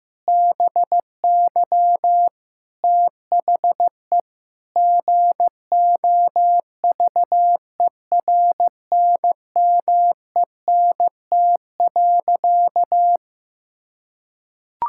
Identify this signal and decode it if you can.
Morse code, heard as a keyed tone, about 15 wpm, 700 Hz. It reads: BY THE GOVERNMENT.
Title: Sentences from Top 1000 Words - Review Part 6 - 15wpm